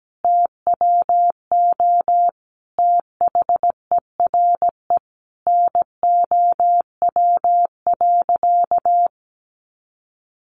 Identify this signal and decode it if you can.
Morse code, heard as a keyed tone, about 17 wpm, 700 Hz. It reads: TWO THERE NOW.
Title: Sentences from Top 200 Words 17wpm